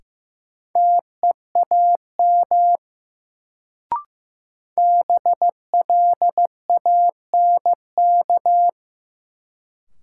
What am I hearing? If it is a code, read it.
Morse code, heard as a keyed tone, about 15 wpm, 700 Hz. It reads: TEAM BLANK
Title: Sentences from Top 1500 Words - New Word Drill Alt - 15wpm